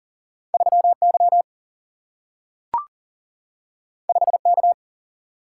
Morse code, heard as a keyed tone, about 39 wpm, 700 Hz. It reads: 3Y 5K
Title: Single-Letter Single-Number 40wpm